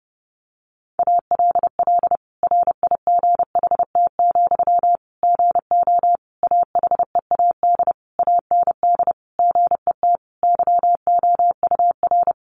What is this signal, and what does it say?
Morse code, heard as a keyed tone, about 30 wpm, 700 Hz. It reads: ALL RIGHT, GO AHEAD AND GET YOUR
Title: Sentences from Top 1100 Words 30wpm